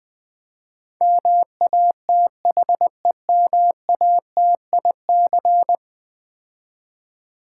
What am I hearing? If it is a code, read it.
Morse code, heard as a keyed tone, about 20 wpm, 700 Hz. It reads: MATHEMATIC
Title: Build A Word - 5 Letters At A Time - 12-Letter Words - 20wpm